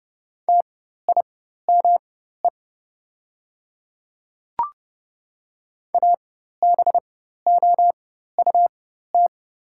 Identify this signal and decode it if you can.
Morse code, heard as a keyed tone, about 31 wpm, spaced as for 14 wpm, 700 Hz. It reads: TIME ABOUT
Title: 4x Character Spacing - Top 100 Words - 30wpm